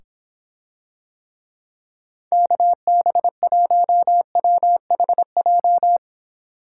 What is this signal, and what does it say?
Morse code, heard as a keyed tone, about 26 wpm, 700 Hz. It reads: KB1WHJ